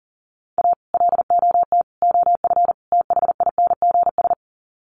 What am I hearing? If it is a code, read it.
Morse code, heard as a keyed tone, about 40 wpm, 700 Hz. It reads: A LOT OF THINGS
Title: Sets of 4 Words 40wpm